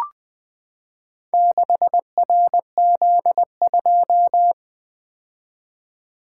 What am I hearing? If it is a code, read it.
Morse code, heard as a keyed tone, about 20 wpm, 700 Hz. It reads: 6RZ2
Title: Four Letter-Number Combinations - 20wpm